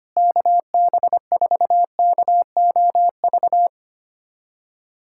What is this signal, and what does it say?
Morse code, heard as a keyed tone, about 25 wpm, 700 Hz. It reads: KB4KOV
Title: USA Callsign DE Callsign - 6 Characters - 25wpm